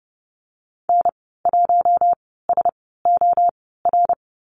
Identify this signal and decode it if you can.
Morse code, heard as a keyed tone, about 30 wpm, spaced as for 17 wpm, 700 Hz. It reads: N1SOR
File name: Lesson 15g - Number 5 - Call Signs 30wpm